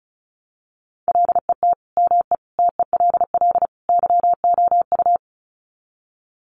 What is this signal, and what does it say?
Morse code, heard as a keyed tone, about 34 wpm, 700 Hz. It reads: LET ME TELL YOU
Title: Sets of 4 Words 35wpm